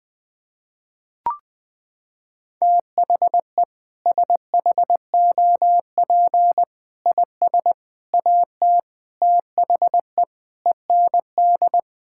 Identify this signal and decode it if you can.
Morse code, heard as a keyed tone, about 20 wpm, 700 Hz. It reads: THE SHOP IS AT THE END
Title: Sentences with 5-Word Phrases - 20wpm